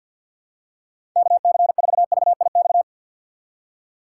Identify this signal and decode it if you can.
Morse code, heard as a keyed tone, about 52 wpm, 700 Hz. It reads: KC4VIX
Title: USA Call Signs - 6 Characters - 3x Character Spacing - 50wpm